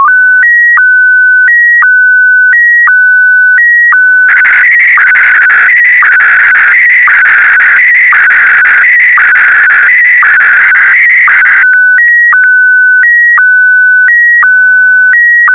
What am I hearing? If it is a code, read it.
SSTV (slow-scan television), Scottie DX.